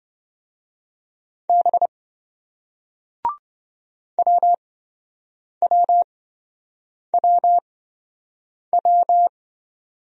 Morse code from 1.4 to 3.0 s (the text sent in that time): B